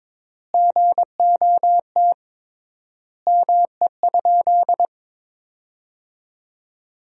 Morse code, heard as a keyed tone, about 22 wpm, 700 Hz. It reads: GOT ME?
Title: 3x Word-Spacing - Sentences from Top 300 Words - 22wpm